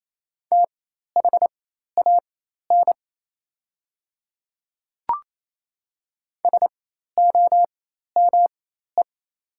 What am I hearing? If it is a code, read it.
Morse code, heard as a keyed tone, about 28 wpm, spaced as for 13 wpm, 690 Hz. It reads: THAN SOME
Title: 4x Character Spacing - Top 200 Words - 28wpm